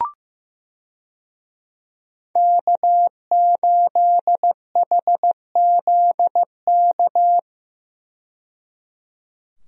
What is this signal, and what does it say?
Morse code, heard as a keyed tone, about 15 wpm, 700 Hz. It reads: K8HZK
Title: USA Call Signs - 5 Characters - 15wpm